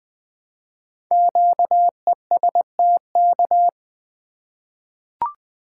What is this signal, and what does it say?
Morse code, heard as a keyed tone, about 20 wpm, 700 Hz. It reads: QESTK
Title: Code Groups - 5 Letters - 20wpm